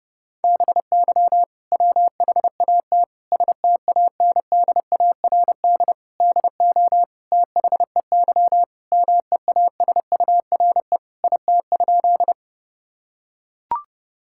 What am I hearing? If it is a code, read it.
Morse code, heard as a keyed tone, about 30 wpm, 700 Hz. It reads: BY WHAT STANDARD DO THEY MEASURE IT?
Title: Sentences from Top 1000 Words - Review Part 6 - 30wpm